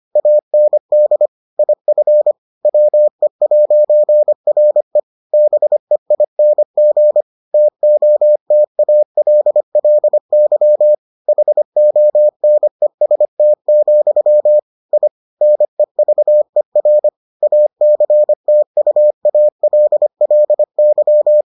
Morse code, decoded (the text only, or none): AND IF WE'RE BEING TOTALLY HONEST, I NEVER ACTUALLY